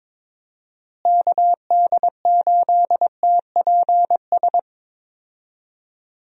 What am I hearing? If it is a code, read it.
Morse code, heard as a keyed tone, about 22 wpm, 700 Hz. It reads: KD8TPS
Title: USA Call Signs - 6 Characters - Repeated 2x - 22wpm